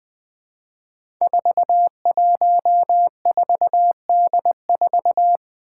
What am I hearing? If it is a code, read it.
Morse code, heard as a keyed tone, about 20 wpm, 700 Hz. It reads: 414D4
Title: Five Challenging Letter-Number Combinations 20wpm